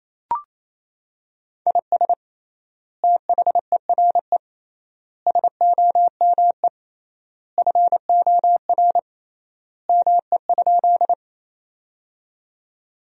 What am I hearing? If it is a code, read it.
Morse code, heard as a keyed tone, about 28 wpm, 700 Hz. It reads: IS THERE SOME FOR ME?
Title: 3x Word-Spacing - Sentences from Top 200 Words - 28wpm